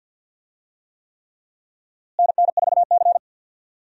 Morse code, heard as a keyed tone, about 47 wpm, 700 Hz. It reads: NN4C
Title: USA Call Signs - 4 Characters - Repeated 3x - 50wpm